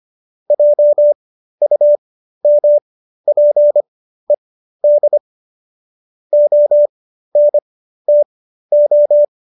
Morse code, heard as a keyed tone, about 25 wpm, spaced as for 13 wpm, 590 Hz. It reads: JUMPED ONTO